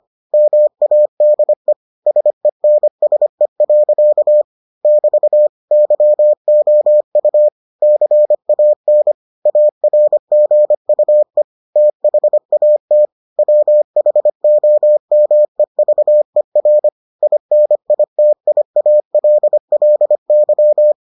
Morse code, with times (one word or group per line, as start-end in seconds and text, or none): 0.3-1.7 MADE
2.1-4.4 SENSE.
4.8-5.5 <BT>
5.7-7.5 YOU
7.8-9.1 CAN
9.5-11.4 ARGUE
11.8-13.1 THAT
13.4-16.9 WHOMEVER
17.2-20.9 INITIALLY